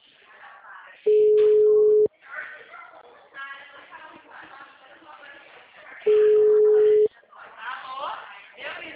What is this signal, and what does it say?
Ringback tone (Europe, 425 Hz), with speech.
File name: out- -2004- - - .41512